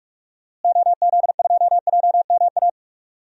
Morse code, heard as a keyed tone, about 46 wpm, 700 Hz. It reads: OZ2JMA
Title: DXCC-100 Call Signs - 6 Characters - Repeated 3x - 45wpm